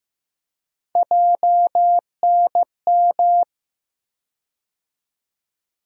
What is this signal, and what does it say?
Morse code, heard as a keyed tone, about 15 wpm, 700 Hz. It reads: JNM
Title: USA Call Sign Postfixes - 3 Letters 15wpm